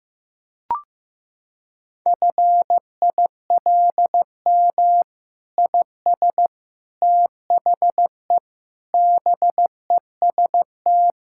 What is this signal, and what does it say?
Morse code, heard as a keyed tone, about 15 wpm, 700 Hz. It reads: FILM IS THE BEST
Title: Sentences from Top 1000 Words - Review Part 6 - 15wpm